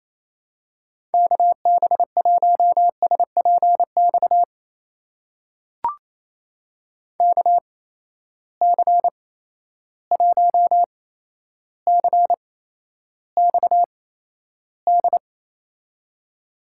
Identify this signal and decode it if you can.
Morse code, heard as a keyed tone, about 28 wpm, spaced as for 12 wpm, 700 Hz. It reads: KB1SPX KC1CXD
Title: USA Call Signs - 6 Characters - 8x Character Spacing - 28wpm